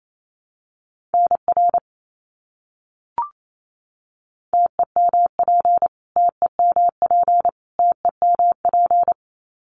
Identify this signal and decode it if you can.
Morse code, heard as a keyed tone, about 28 wpm, 700 Hz. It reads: NR TEMP TEMP TEMP
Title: Abbreviations Repeated 3x - 28wpm